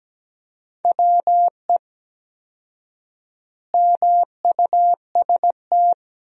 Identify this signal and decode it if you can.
Morse code, heard as a keyed tone, about 17 wpm, 700 Hz. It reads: WE MUST
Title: 4x Word-Spacing - Sentences from Top 200 Words - 17wpm